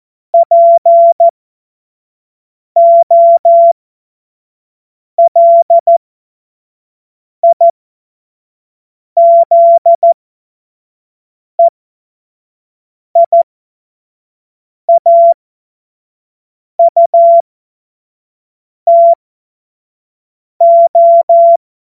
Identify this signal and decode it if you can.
Morse code, heard as a keyed tone, about 14 wpm, spaced as for 5 wpm, 680 Hz. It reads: POLIZEIAUTO
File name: Morsekurs / Witze / Witz 30. August 2022 25 F70